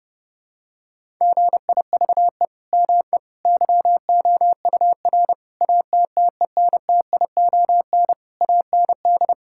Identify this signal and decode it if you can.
Morse code, heard as a keyed tone, about 30 wpm, 700 Hz. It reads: GIVE ME YOUR ATTENTION AND
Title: Sentences from Top 1000 Words - Review Part 6 - 30wpm